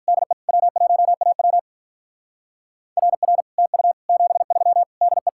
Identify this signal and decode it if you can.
Morse code, heard as a keyed tone, about 43 wpm, 700 Hz. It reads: DE W1AW RR TU 73 DE
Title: Speed-Racing Common QSO Phrases - 35wpm